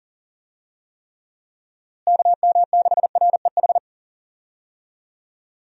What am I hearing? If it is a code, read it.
Morse code, heard as a keyed tone, about 40 wpm, 700 Hz. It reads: KM6REH
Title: USA Call Signs - 6 Characters - Repeated 2x - 40wpm